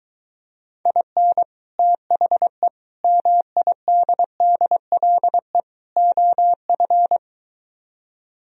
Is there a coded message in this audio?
Yes: Morse code from 0.9 to 7.2 s, heard as a keyed tone, about 23 wpm, 700 Hz. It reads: IN THE MIDDLE OF